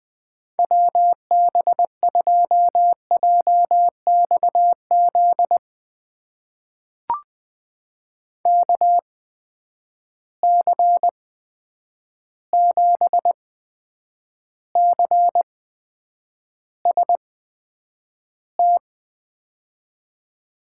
Morse code, heard as a keyed tone, about 20 wpm, spaced as for 9 wpm, 700 Hz. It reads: WB2JXZ KC7CST
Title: USA Call Signs - 8x Character Spacing - 20wpm